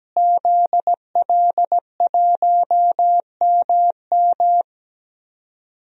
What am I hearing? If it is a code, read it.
Morse code, heard as a keyed tone, about 17 wpm, 700 Hz. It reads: ZL1MM
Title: DXCC-100 Call Signs - 5 Characters - Repeated 3x - 17wpm